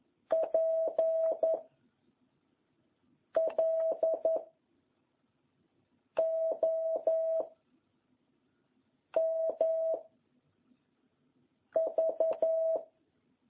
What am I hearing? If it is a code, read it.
Morse code, heard as a keyed tone, about 11 wpm, 660 Hz. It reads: P L O M V